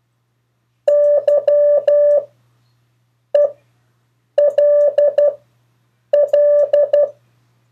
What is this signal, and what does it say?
Morse code, heard as a keyed tone, about 12 wpm, spaced as for 6 wpm, 580 Hz. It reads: YELL
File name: 2of4-words-1